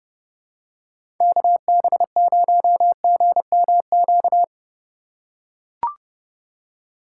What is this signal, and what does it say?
Morse code, heard as a keyed tone, about 32 wpm, 700 Hz. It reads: KB0GMQ
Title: USA Call Signs - 6 Characters - 3x Character Spacing - 30wpm